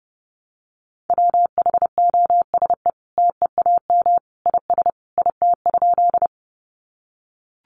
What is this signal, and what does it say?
Morse code, heard as a keyed tone, about 28 wpm, 700 Hz. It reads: WHOSE TEAM IS IT?
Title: Sentences from Top 1500 Words 30wpm